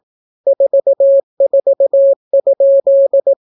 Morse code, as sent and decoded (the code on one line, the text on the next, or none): ....- ....- ..--..
44?